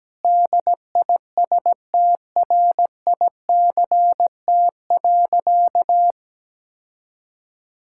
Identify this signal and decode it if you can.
Morse code, heard as a keyed tone, about 17 wpm, 700 Hz. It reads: DISTRICT.
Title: Sentences from Top 1600 Words 17wpm